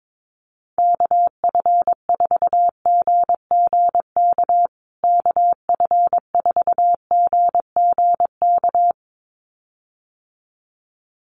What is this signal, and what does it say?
Morse code, heard as a keyed tone, about 22 wpm, 700 Hz. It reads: KF4GGK KF4GGK
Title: USA Call Signs Repeated 2x 22wpm